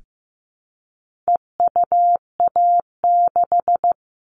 Morse code, heard as a keyed tone, about 15 wpm, 700 Hz. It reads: EUA6
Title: All Cut Numbers - 4-Digit 15wpm